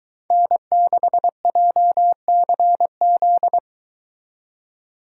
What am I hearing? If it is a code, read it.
Morse code, heard as a keyed tone, about 23 wpm, 700 Hz. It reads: N6JCZ